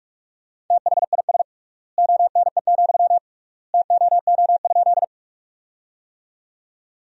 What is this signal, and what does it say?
Morse code, heard as a keyed tone, about 47 wpm, 700 Hz. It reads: THIS ONE, TOO?
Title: 3x Word-Spacing - Sentences from Top 200 Words - 45wpm